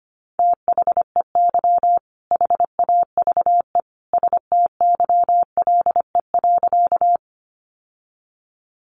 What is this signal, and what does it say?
Morse code, heard as a keyed tone, about 25 wpm, 700 Hz. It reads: THEY HAVE STYLE.
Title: Sentences from Top 800 Words 25wpm